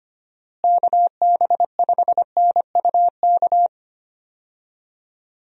Morse code, heard as a keyed tone, about 25 wpm, 700 Hz. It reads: KB5NUK
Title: USA Call Signs - 6 Characters - Repeated 2x - 25wpm